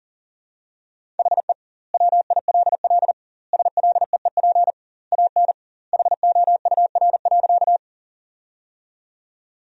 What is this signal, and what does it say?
Morse code, heard as a keyed tone, about 40 wpm, 700 Hz. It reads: HE WILL SLEEP AN HOUR.
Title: 2x Word-Spacing - Sentences from Top 500 Words - 40wpm